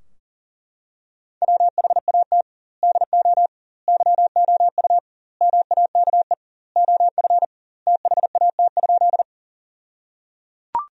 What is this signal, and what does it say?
Morse code, heard as a keyed tone, about 37 wpm, 700 Hz. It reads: WHAT DO YOU MAKE OF THAT?
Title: 2x Word-Spacing - Sentences from Top 100 Words - 40wpm